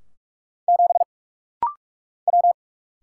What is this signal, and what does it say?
Morse code, heard as a keyed tone, about 44 wpm, 700 Hz. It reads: 7 W
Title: Single Letter-Number Alt - Rapid-Fire 45wpm